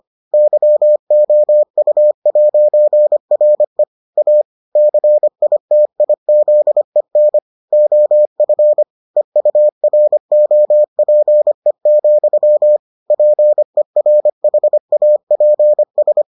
Morse code, heard as a keyed tone, about 25 wpm, 600 Hz. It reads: YOU'RE A CITIZEN OF EUROPE, PERHAPS